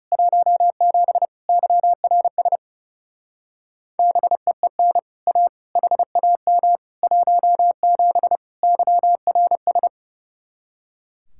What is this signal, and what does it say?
Morse code, heard as a keyed tone, about 36 wpm, 700 Hz. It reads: 17 YRS BEEN A HAM 17 YRS
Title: Speed-Racing Common QSO Phrases Encore - 30wpm